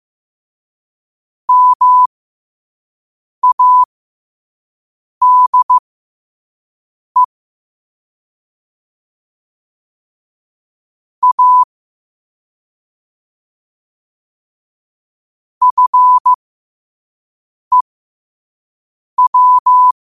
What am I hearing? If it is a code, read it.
Morse code, heard as a keyed tone, about 15 wpm, spaced as for 5 wpm, 1000 Hz. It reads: MADE A FEW
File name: 5wpm